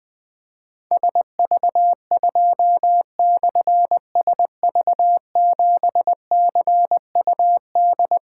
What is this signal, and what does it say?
Morse code, heard as a keyed tone, about 20 wpm, 700 Hz. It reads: SV2/SV7CUD
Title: Difficult Call Signs 20wpm